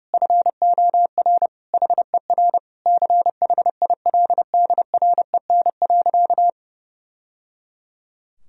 Morse code, heard as a keyed tone, about 32 wpm, 700 Hz. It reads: FOR HER CHILDREN.
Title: Sentences from Top 1000 Words - Review Part 3 - 30wpm